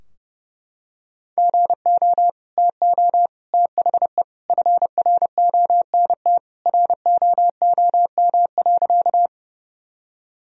Morse code, heard as a keyed tone, about 32 wpm, 700 Hz. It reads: GO TO THE FRONT ROOM.
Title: Sentences from Top 1000 Words - Review Part 2 - 30wpm